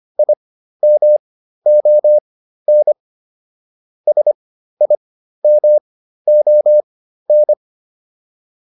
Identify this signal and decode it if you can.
Morse code, heard as a keyed tone, about 25 wpm, spaced as for 13 wpm, 600 Hz. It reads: IMON SIMON
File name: QSO 305 13